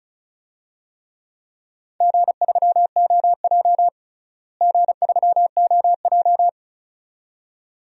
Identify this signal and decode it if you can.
Morse code, heard as a keyed tone, about 36 wpm, 700 Hz. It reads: G3OJ G3OJ
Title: DXCC-100 Call Signs - 4 Characters - Repeated 2x - 35wpm